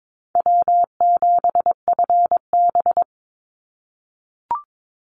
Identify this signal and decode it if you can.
Morse code, heard as a keyed tone, about 22 wpm, 700 Hz. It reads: W7FB
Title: Call Signs - 22wpm